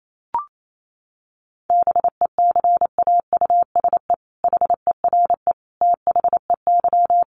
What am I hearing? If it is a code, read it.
Morse code, heard as a keyed tone, about 28 wpm, 700 Hz. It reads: BECAUSE HERE THEY
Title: Sentences from Top 1500 Words 28wpm